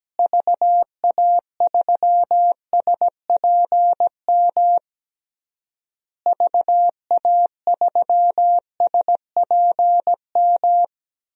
Morse code, heard as a keyed tone, about 17 wpm, 700 Hz. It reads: VA3SPM VA3SPM
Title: DXCC-100 Call Signs - 6 Characters - Repeated 3x - 17wpm